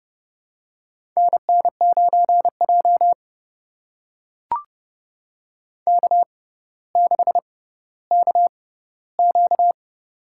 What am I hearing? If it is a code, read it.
Morse code, heard as a keyed tone, about 30 wpm, spaced as for 15 wpm, 700 Hz. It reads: NN9J K6KQ
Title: USA Call Signs - 4 Characters - 6x Character Spacing - 30wpm